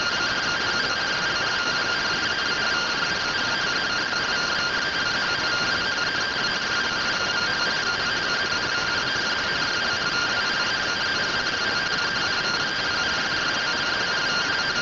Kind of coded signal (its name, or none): two-tone teleprinter signal